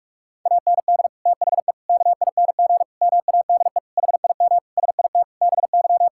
Morse code, heard as a keyed tone, about 46 wpm, 700 Hz. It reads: AND THE KING MADE HIM SIT BY
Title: Sentences from Top 1600 Words 45wpm